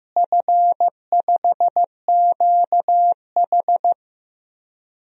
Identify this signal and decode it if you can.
Morse code, heard as a keyed tone, about 15 wpm, 700 Hz. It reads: F5QH
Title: DXCC-100 Call Signs - 4 Characters - Repeated 3x - 15wpm